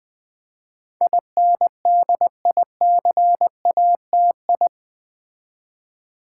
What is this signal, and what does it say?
Morse code, heard as a keyed tone, about 20 wpm, 700 Hz. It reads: INDICATI
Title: Build A Word - 4 Letters At A Time - 10-Letter Words - 20wpm